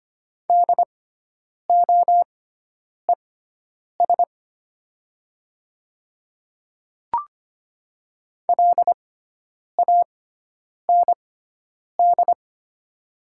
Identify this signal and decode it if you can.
Morse code, heard as a keyed tone, about 25 wpm, spaced as for 9 wpm, 700 Hz. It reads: DOES LAND
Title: 6x Character Spacing - Top 200 Words - 25wpm